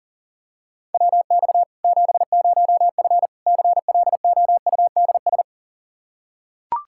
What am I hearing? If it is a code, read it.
Morse code, heard as a keyed tone, about 40 wpm, 700 Hz. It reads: WX 70F CLOUDS
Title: Practice QSO - Exchange 2 - Info - 40wpm